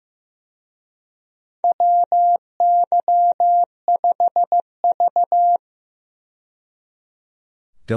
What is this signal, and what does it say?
Morse code, heard as a keyed tone, about 15 wpm, 700 Hz. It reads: WY5V
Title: USA Call Signs - 4 Characters - 15wpm